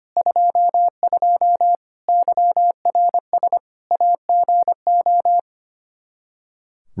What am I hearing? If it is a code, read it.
Morse code, heard as a keyed tone, about 25 wpm, 700 Hz. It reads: 22 YRS AGO
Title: Practice QSO - Exchange 3 - More Info - 25wpm